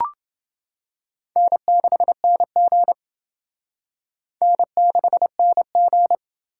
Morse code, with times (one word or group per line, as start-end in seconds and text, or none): 1.4-2.9 N6NG
4.4-6.2 N6NG